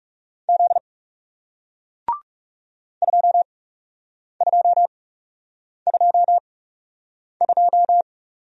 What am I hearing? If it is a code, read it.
Morse code, heard as a keyed tone, about 40 wpm, spaced as for 8 wpm, 700 Hz. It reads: Z 2222